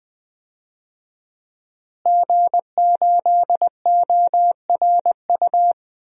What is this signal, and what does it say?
Morse code, heard as a keyed tone, about 21 wpm, 700 Hz. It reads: G8ORU
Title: DXCC-100 Call Signs - 5 Characters - Repeated 2x - 20wpm